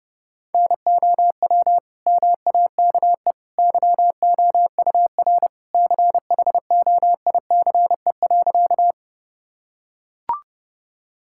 Morse code, heard as a keyed tone, about 30 wpm, 700 Hz. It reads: NOW MAKE YOUR CHOICE.
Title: Sentences from Top 1000 Words - Review Part 6 - 30wpm